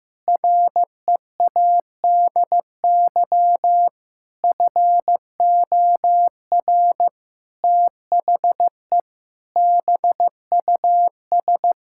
Morse code, heard as a keyed tone, about 15 wpm, 700 Hz. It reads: READY FOR THE BUS
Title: Sentences from Top 1000 Words - Review Part 6 - 15wpm